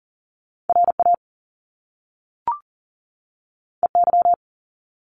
Morse code, heard as a keyed tone, about 39 wpm, 700 Hz. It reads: RA EY
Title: Top Two Letter Combinations in Words 40wpm